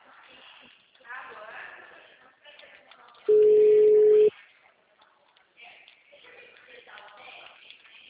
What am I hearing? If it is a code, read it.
Ringback tone (Europe, 425 Hz), with speech.